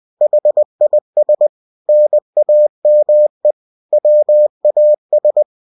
Morse code, heard as a keyed tone, about 20 wpm, 600 Hz. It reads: HIS NAME WAS
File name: cw-books / winnie-the-pooh / winnie-20wpm-page-0082